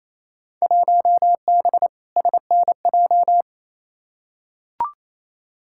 Morse code, heard as a keyed tone, about 28 wpm, 700 Hz. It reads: 1B SNJ
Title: Field Day Exchanges Encore - 28wpm